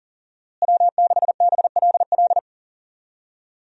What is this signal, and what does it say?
Morse code, heard as a keyed tone, about 40 wpm, 700 Hz. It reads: W6BLL
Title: USA Call Signs - 5 Characters - Repeated 2x - 40wpm